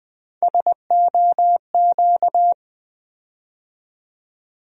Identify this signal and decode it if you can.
Morse code, heard as a keyed tone, about 20 wpm, 700 Hz. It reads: SOQ
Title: USA Call Sign Postfixes - 3 Letters 20wpm